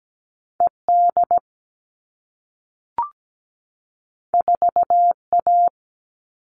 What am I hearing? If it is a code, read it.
Morse code, heard as a keyed tone, about 17 wpm, 700 Hz. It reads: ED 4A
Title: All Cut Numbers - 2-Digit 17wpm